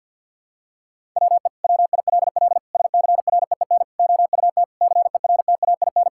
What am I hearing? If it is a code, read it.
Morse code, heard as a keyed tone, about 50 wpm, 700 Hz. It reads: WE WILL SCREEN OUT CERTAIN
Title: Sentences from Top 1000 Words - Review Part 6 - 50wpm